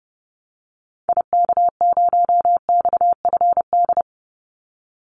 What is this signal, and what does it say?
Morse code, heard as a keyed tone, about 30 wpm, 700 Hz. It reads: IK0XFD
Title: Call Signs - 30wpm